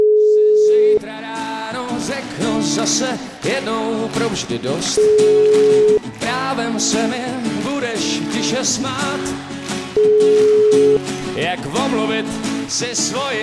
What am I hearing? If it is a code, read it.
Ringback tone (Europe, 425 Hz), with speech.